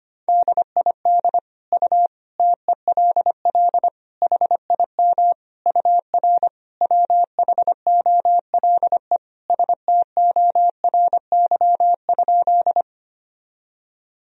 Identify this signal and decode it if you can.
Morse code, heard as a keyed tone, about 25 wpm, 700 Hz. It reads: DID U TELL HIM UR WHOLE STORY?
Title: Sentences from Top 400 Words and Abbreviations 25wpm